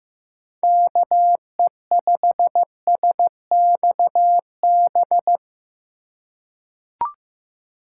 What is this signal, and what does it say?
Morse code, heard as a keyed tone, about 15 wpm, 700 Hz. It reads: KE5SXB
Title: USA Call Signs - 6 Characters - 4x Character Spacing - 15wpm